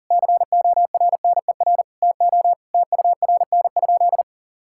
Morse code, heard as a keyed tone, about 40 wpm, 700 Hz. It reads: CORNER TO TURN?
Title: Sentences from Top 1600 Words 40wpm